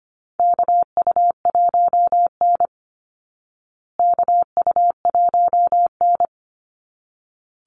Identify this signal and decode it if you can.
Morse code, heard as a keyed tone, about 25 wpm, 700 Hz. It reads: KU1N KU1N
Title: Call Signs Repeated 3x - 25wpm